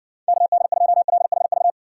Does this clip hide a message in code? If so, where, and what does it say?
Yes: Morse code from 0.3 to 1.7 s, heard as a keyed tone, about 60 wpm, 700 Hz. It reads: KD2LHV